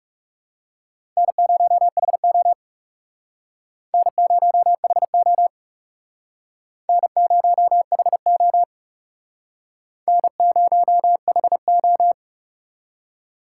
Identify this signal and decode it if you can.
Morse code, heard as a keyed tone, about 38 wpm, 700 Hz. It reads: N0HO N0HO N0HO N0HO